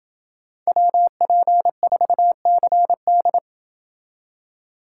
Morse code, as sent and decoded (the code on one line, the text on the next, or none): .-- .--. ....- -.-. -..
WP4CD